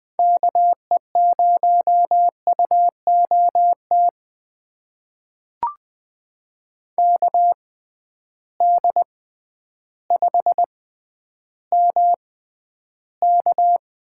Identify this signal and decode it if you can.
Morse code, heard as a keyed tone, about 21 wpm, spaced as for 11 wpm, 700 Hz. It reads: KE0UOT KD5MK
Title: USA Call Signs - 6 Characters - 6x Character Spacing - 20wpm